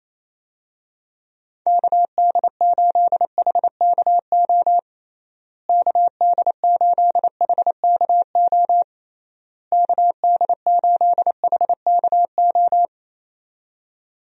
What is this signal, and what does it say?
Morse code, heard as a keyed tone, about 28 wpm, 700 Hz. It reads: KD8HKO KD8HKO KD8HKO
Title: USA Call Signs - 6 Characters - Repeated 3x - 28wpm